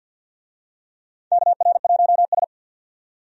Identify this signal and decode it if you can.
Morse code, heard as a keyed tone, about 50 wpm, 700 Hz. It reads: KR1S
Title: USA Call Signs - 4 Characters - 50wpm